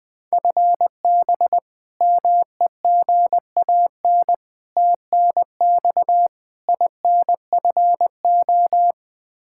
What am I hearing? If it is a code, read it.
Morse code, heard as a keyed tone, about 20 wpm, 700 Hz. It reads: FB MEGAN TNX INFO
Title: Practice QSO - Exchange 3 - More Info - 20wpm